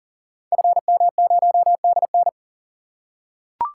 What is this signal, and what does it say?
Morse code, heard as a keyed tone, about 39 wpm, 690 Hz. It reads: FM0DN
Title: Code Groups - 5 Letter:Numbers - 40wpm